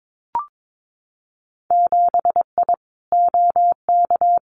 Morse code, heard as a keyed tone, about 22 wpm, 700 Hz. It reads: 7I OK
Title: Winter Field Day Exchanges Encore 22wpm